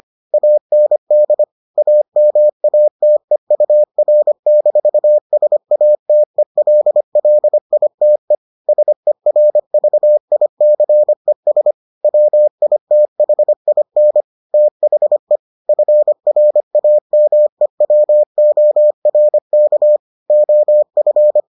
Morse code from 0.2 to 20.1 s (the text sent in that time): AND AMATEUR-SATELLITE SERVICES WITHIN THE FRAMEWORK